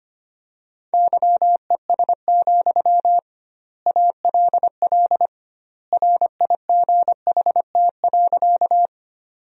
Morse code, heard as a keyed tone, about 25 wpm, 700 Hz. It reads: YES, ALL RIGHT.
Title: 2x Word-Spacing - Sentences from Top 500 Words - 25wpm